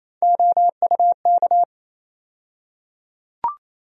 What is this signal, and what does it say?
Morse code, heard as a keyed tone, about 28 wpm, 700 Hz. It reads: OUK
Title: USA Call Sign Postfixes - 3 Letters 28wpm